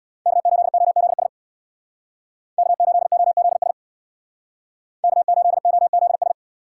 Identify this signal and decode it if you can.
Morse code, heard as a keyed tone, about 61 wpm, 700 Hz. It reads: K8OZS K8OZS K8OZS
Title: Speed Racing - USA Call Signs 50wpm